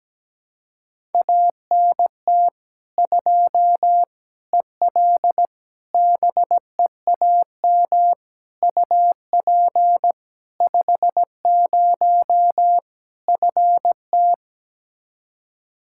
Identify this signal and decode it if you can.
Morse code, heard as a keyed tone, about 17 wpm, 700 Hz. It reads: ANT 2 EL BEAM UP 50 FT
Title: Practice QSO - Exchange 2 - Info - 17wpm